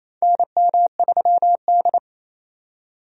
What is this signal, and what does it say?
Morse code, heard as a keyed tone, about 27 wpm, 700 Hz. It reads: NM3D
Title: USA Call Signs - 4 Characters - 28wpm